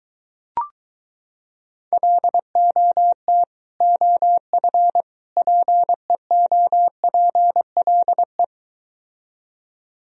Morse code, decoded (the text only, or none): LOT OF PEOPLE